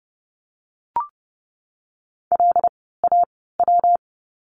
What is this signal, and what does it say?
Morse code, heard as a keyed tone, about 30 wpm, spaced as for 17 wpm, 700 Hz. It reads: LAW
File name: course / 30:10wpm / 115 Lesson 19f - Letter w - Words 30wpm